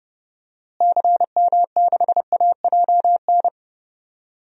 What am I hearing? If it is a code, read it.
Morse code, heard as a keyed tone, about 32 wpm, 700 Hz. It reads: CM6AJN
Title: DXCC-100 Call Signs - 6 Characters - Repeated 3x - 30wpm